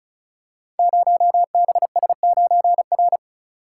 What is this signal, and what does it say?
Morse code, heard as a keyed tone, about 36 wpm, 690 Hz. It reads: 0BS9R
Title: Code Groups - 5 Letter:Numbers - 35wpm